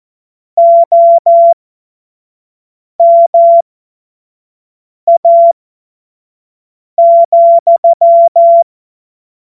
Morse code, heard as a keyed tone, about 14 wpm, spaced as for 5 wpm, 680 Hz. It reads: OMA,
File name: Morsekurs / Witze / Witz 19. Oktober 2021, 25 F70